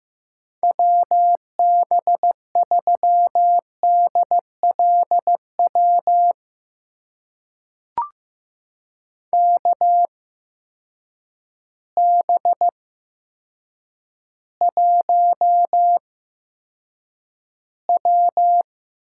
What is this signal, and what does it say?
Morse code, heard as a keyed tone, about 15 wpm, spaced as for 8 wpm, 700 Hz. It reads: WB3DLW KB1W
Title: USA Call Signs - 6 Characters - 8x Character Spacing - 15wpm